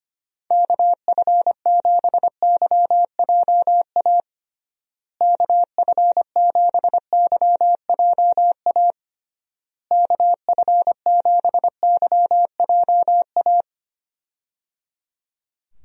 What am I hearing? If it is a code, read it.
Morse code, heard as a keyed tone, about 25 wpm, 700 Hz. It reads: KF7YJA KF7YJA KF7YJA
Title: USA Call Signs - 6 Characters - Repeated 3x - 25wpm